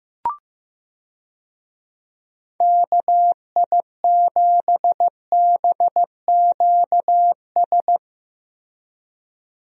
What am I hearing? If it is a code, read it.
Morse code, heard as a keyed tone, about 15 wpm, 700 Hz. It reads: KI7BQS
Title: USA Call Signs - 6 Characters - Repeated 3x - 15wpm